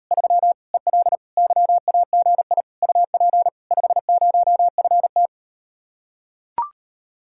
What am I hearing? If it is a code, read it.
Morse code, heard as a keyed tone, about 38 wpm, 700 Hz. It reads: 3 EL YAGI UP 50FT